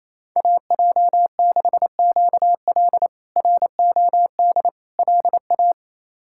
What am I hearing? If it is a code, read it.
Morse code, heard as a keyed tone, about 27 wpm, 700 Hz. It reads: AJ6QL ROD LA
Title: K1USN SST QSO Practice - 28wpm